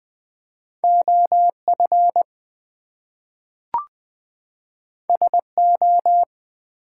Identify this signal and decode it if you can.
Morse code, heard as a keyed tone, about 20 wpm, 700 Hz. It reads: OF SO
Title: Two Letter Words - 20wpm